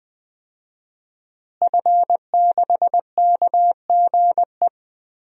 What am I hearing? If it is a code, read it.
Morse code, heard as a keyed tone, about 20 wpm, 700 Hz. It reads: F6KGE
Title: DXCC-100 Call Signs - 5 Characters - Repeated 2x - 20wpm